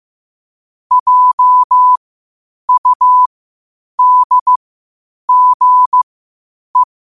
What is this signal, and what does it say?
Morse code, heard as a keyed tone, about 15 wpm, spaced as for 8 wpm, 1000 Hz. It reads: JUDGE
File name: qotdfiles / 8wpm